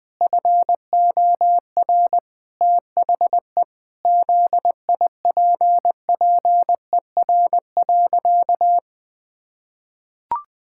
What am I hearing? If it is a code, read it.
Morse code, heard as a keyed tone, about 21 wpm, 700 Hz. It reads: FOR THE ZIPPER.
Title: Pangram Challenge v2 - 20wpm